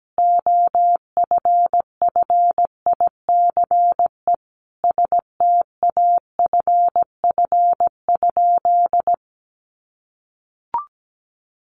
Morse code, heard as a keyed tone, about 17 wpm, 700 Hz. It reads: OFFICE STAFF?
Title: Sentences from Top 1200 Words 17wpm